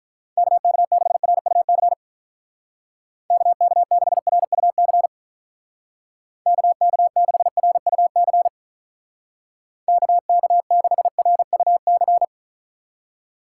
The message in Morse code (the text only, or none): KK6RUC KK6RUC KK6RUC KK6RUC